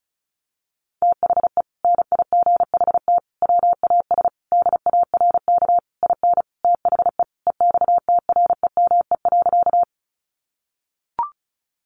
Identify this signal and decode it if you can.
Morse code, heard as a keyed tone, about 34 wpm, 700 Hz. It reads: THE NIGHT WAS DARK IN THE EXTREME.
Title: Sentences from Top 1300 Words 35wpm